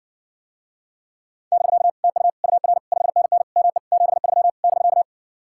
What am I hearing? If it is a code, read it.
Morse code, heard as a keyed tone, about 59 wpm, 700 Hz. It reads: <BK> TU UR 5NN ME 73 <BK>